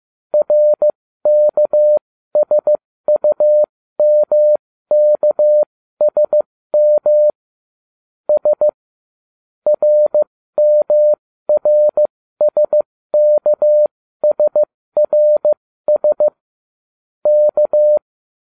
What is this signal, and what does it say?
Morse code, heard as a keyed tone, about 15 wpm, spaced as for 13 wpm, 600 Hz. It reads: RKSUMKSM S RMRSKSRS K